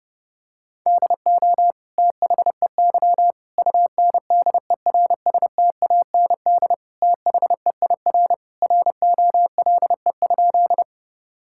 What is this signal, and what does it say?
Morse code, heard as a keyed tone, about 30 wpm, 700 Hz. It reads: DO THEY UNDERSTAND THEIR ROLE?
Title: Sentences from Top 1000 Words - Review Part 6 - 30wpm